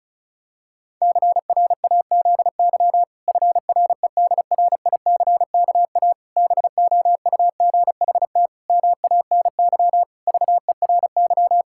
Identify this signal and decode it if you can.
Morse code, heard as a keyed tone, about 36 wpm, 700 Hz. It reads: CRAZY FREDRICKA BOUGHT MANY VERY
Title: Pangram Challenge v2 - 35wpm